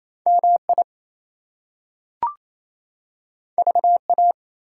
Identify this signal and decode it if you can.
Morse code, heard as a keyed tone, about 28 wpm, 700 Hz. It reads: MI VA
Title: US States and Canadian Provinces 28wpm